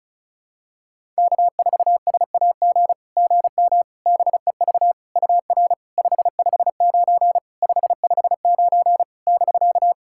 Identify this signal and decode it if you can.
Morse code, heard as a keyed tone, about 36 wpm, 690 Hz. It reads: K4SAG GM BEV UR 559 559 <BK>
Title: SOTA QSO Practice 35wpm